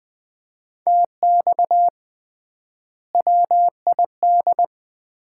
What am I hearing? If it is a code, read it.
Morse code, heard as a keyed tone, about 20 wpm, 700 Hz. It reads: TX WID
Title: Common QSO Phrases - 3x Word-Spacing - 20wpm